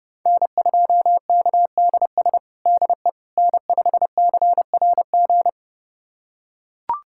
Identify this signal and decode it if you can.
Morse code, heard as a keyed tone, about 30 wpm, 700 Hz. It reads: N2KDS DE N5CRG
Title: USA Callsign DE Callsign - 5 Characters - 30wpm